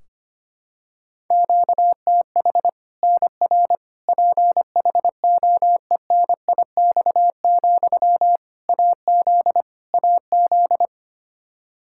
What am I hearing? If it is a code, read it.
Morse code, heard as a keyed tone, about 25 wpm, 700 Hz. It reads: QTH NR PHOENIX, AZ AZ
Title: K1USN SST QSO Practice 25wpm